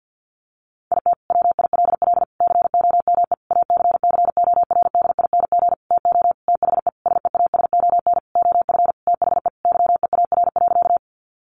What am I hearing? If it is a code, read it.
Morse code, heard as a keyed tone, about 50 wpm, 700 Hz. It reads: IT WILL COME ACCORDING TO THE SEASON OF THE YEAR.
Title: Sentences from Top 1400 Words 50wpm